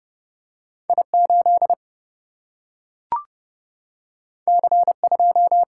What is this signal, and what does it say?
Morse code, heard as a keyed tone, about 30 wpm, 700 Hz. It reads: I 8 C 2
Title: Single-Letter Single-Number 30wpm